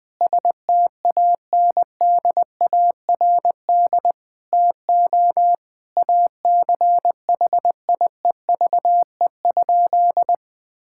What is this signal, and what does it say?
Morse code, heard as a keyed tone, about 20 wpm, 700 Hz. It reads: STANDARD TO ACHIEVE?
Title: Sentences from Top 1000 Words - Review Part 6 - 20wpm